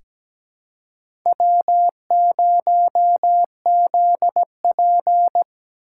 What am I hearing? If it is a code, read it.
Morse code, heard as a keyed tone, about 17 wpm, 700 Hz. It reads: W0ZP
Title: K1USN SST QSO Practice - 17wpm